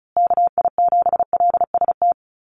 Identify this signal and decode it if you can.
Morse code, heard as a keyed tone, about 34 wpm, 700 Hz. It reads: KI7LST
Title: USA Call Signs Repeated 2x 35wpm